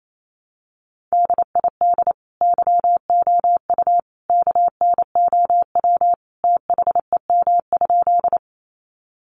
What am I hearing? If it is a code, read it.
Morse code, heard as a keyed tone, about 28 wpm, 700 Hz. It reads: DID YOU KNOW THEM?
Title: Sentences from Top 100 Words 28wpm